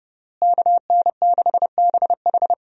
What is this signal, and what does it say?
Morse code, heard as a keyed tone, about 30 wpm, 700 Hz. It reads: KN6BH
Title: USA Call Signs - 5 Characters - Repeated 2x - 30wpm